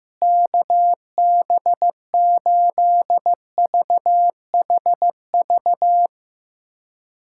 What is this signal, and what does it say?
Morse code, heard as a keyed tone, about 15 wpm, 700 Hz. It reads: KB8VHV
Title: USA Callsign DE Callsign - 6 Characters - 15wpm